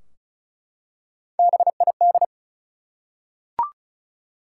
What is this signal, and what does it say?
Morse code, heard as a keyed tone, about 34 wpm, 700 Hz. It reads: BID
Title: Three Letter Words - 35wpm